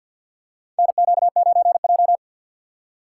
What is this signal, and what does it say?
Morse code, heard as a keyed tone, about 52 wpm, 700 Hz. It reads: NQ9J
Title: USA Call Signs - 2x Character Spacing - 50wpm